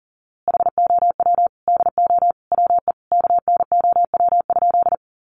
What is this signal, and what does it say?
Morse code, heard as a keyed tone, about 37 wpm, 700 Hz. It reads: HOW DO WE KNOW?
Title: A Sound Of Thunder 40wpm